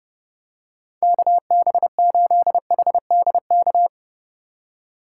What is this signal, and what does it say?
Morse code, heard as a keyed tone, about 29 wpm, 700 Hz. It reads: KB8HDK
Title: USA Call Signs - 6 Characters - 3x Character Spacing - 30wpm